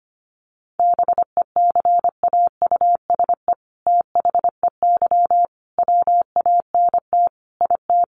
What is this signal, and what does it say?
Morse code, heard as a keyed tone, about 25 wpm, 700 Hz. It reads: BECAUSE THEY WANT IT
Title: Sentences from Top 800 Words 25wpm